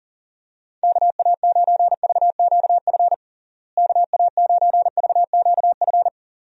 Morse code, heard as a keyed tone, about 40 wpm, 700 Hz. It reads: KA9VQF KA9VQF
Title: USA Call Signs - 6 Characters - Repeated 2x - 40wpm